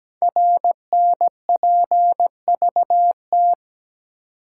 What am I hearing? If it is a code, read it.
Morse code, heard as a keyed tone, about 17 wpm, 700 Hz. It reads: RNPVT
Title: Code Groups - 5 Letters - 17wpm